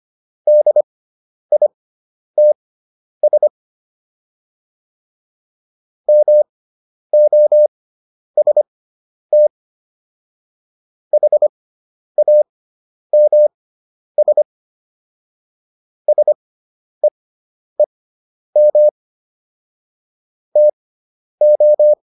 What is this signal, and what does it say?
Morse code, heard as a keyed tone, about 25 wpm, spaced as for 10 wpm, 600 Hz. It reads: DITS MOST HAMS SEEM TO